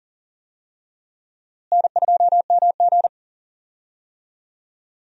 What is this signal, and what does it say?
Morse code, heard as a keyed tone, about 40 wpm, 700 Hz. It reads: N2MG
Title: USA Call Signs - 4 Characters - Repeated 3x - 40wpm